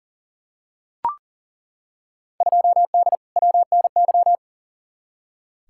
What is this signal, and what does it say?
Morse code, heard as a keyed tone, about 40 wpm, 700 Hz. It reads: 2D WNY
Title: Field Day Exchanges - 40wpm